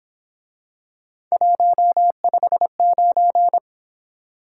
Morse code, heard as a keyed tone, about 27 wpm, 700 Hz. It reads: 159
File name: Speed-Racing - Three Letter-Number Combinations - 22wpm